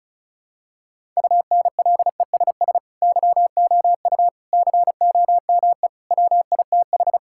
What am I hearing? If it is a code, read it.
Morse code, heard as a keyed tone, about 36 wpm, 700 Hz. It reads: UNLESS YOU COME WITH
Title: Sentences from Top 1000 Words - Review Part 6 - 35wpm